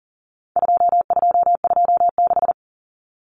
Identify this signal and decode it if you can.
Morse code, heard as a keyed tone, about 40 wpm, 700 Hz. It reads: 2226
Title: Common Cut Numbers - 4-Digit 40wpm